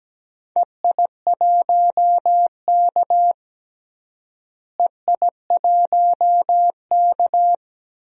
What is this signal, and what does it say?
Morse code, heard as a keyed tone, about 17 wpm, 700 Hz. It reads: EI1K EI1K
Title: DXCC-100 Call Signs - 4 Characters - Repeated 3x - 17wpm